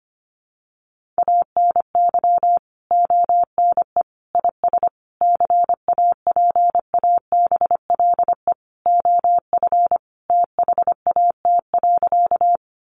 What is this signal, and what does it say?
Morse code, heard as a keyed tone, about 25 wpm, 700 Hz. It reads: ANY ONE IS CAPABLE OF THAT.
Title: Sentences from Top 1200 Words 25wpm